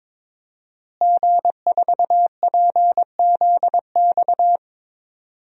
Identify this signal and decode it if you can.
Morse code, heard as a keyed tone, about 22 wpm, 700 Hz. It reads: G4PZX
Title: DXCC-100 Call Signs - 5 Characters - Repeated 3x - 22wpm